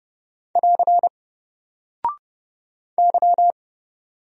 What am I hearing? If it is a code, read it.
Morse code, heard as a keyed tone, about 29 wpm, 700 Hz. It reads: <AR> Y
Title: Single Character (Letter:Number:Punctuation:Prosign) 30wpm